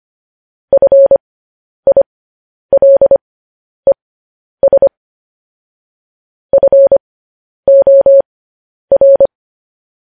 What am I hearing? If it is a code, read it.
Morse code, heard as a keyed tone, about 25 wpm, spaced as for 10 wpm, 550 Hz. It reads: FILES FOR
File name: podcast / News headlines-1x-25x10